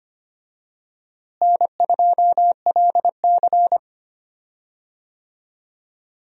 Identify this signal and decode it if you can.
Morse code, heard as a keyed tone, about 25 wpm, 700 Hz. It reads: N2LC
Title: USA Call Signs - 4 Characters - Repeated 3x - 25wpm